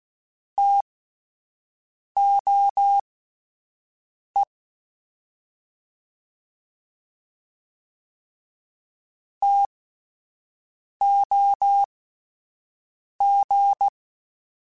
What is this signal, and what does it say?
Morse code, heard as a keyed tone, about 16 wpm, spaced as for 5 wpm, 800 Hz. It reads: TOE TOG